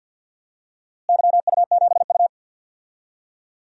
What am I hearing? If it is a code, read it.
Morse code, heard as a keyed tone, about 50 wpm, 700 Hz. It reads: YU7U
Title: DXCC-100 Call Signs - 4 Characters - Repeated 2x - 50wpm